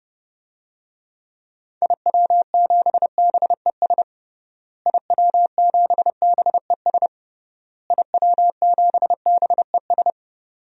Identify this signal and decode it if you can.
Morse code, heard as a keyed tone, about 30 wpm, 700 Hz. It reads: IW7BES IW7BES IW7BES
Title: DXCC-100 Call Signs - 6 Characters - Repeated 3x - 30wpm